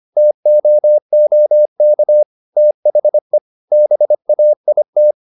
Morse code, heard as a keyed tone, about 25 wpm, 600 Hz. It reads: TOOK THE BAIT
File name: the old man and the sea part 1 25wpm